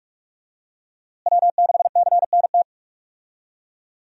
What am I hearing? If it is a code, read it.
Morse code, heard as a keyed tone, about 43 wpm, 700 Hz. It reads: W6CNT
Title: USA Call Signs - 5 Characters - Repeated 2x - 45wpm